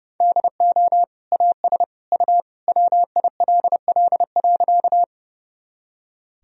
Morse code, heard as a keyed tone, about 32 wpm, 700 Hz. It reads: DO AS U WILL.
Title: Sentences from Top 100 Words and Abbreviations 30wpm